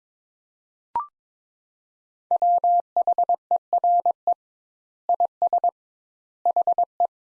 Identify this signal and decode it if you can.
Morse code, heard as a keyed tone, about 22 wpm, 700 Hz. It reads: WHERE IS HE
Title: Sentences Top 200 Words - 22wpm